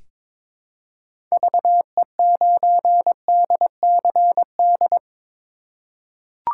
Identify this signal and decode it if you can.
Morse code, heard as a keyed tone, about 22 wpm, 700 Hz. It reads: VE9DCD
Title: K1USN SST QSO Practice - 22wpm